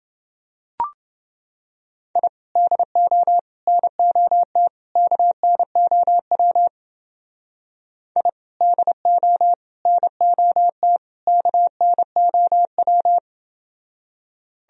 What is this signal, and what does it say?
Morse code, heard as a keyed tone, about 28 wpm, 700 Hz. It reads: I DO NOT KNOW I DO NOT KNOW